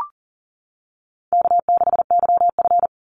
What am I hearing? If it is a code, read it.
Morse code, heard as a keyed tone, about 40 wpm, 700 Hz. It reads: K6YF
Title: Easy Call Signs - 40wpm